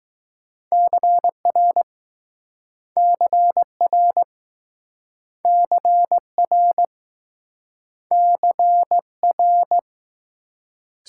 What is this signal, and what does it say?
Morse code, heard as a keyed tone, about 21 wpm, 700 Hz. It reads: CR CR CR CR